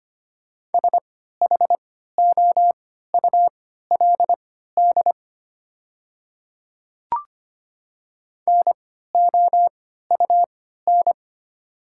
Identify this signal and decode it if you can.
Morse code, heard as a keyed tone, about 25 wpm, spaced as for 14 wpm, 700 Hz. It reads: SHOULD NOUN
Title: 3x Character Spacing - Top 500 Words - 25wpm